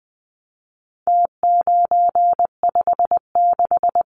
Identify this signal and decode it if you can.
Morse code, heard as a keyed tone, about 20 wpm, 700 Hz. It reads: T956
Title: Numbers - Contest Style - 4-Digit 20wpm